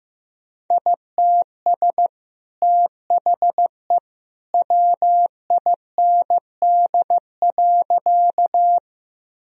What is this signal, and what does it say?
Morse code, heard as a keyed tone, about 15 wpm, 700 Hz. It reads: ITS THE WIND.
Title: Sentences from Top 1000 Words - Review Part 2 - 15wpm